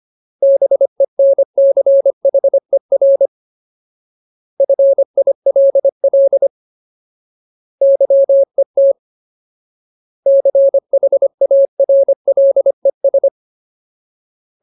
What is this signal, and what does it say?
Morse code, heard as a keyed tone, about 25 wpm, 550 Hz. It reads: BENCHER FILL YET CHARLES
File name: Ultimate Word List / 25WPM Ultimate